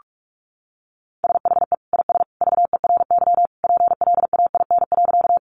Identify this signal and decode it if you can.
Morse code, heard as a keyed tone, about 44 wpm, 700 Hz. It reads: SHE IS VERY PLAIN.
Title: Sentences from Top 400 Words 45wpm